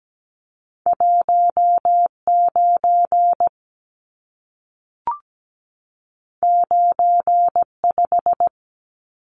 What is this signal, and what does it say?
Morse code, heard as a keyed tone, about 17 wpm, 700 Hz. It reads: 19 95
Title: Numbers - 2-Digit 17wpm